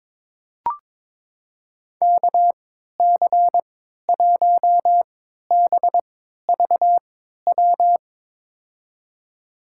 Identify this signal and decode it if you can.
Morse code, heard as a keyed tone, about 22 wpm, spaced as for 13 wpm, 700 Hz. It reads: KC1BVW
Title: USA Call Signs - 6 Characters - 3x Character Spacing - 22wpm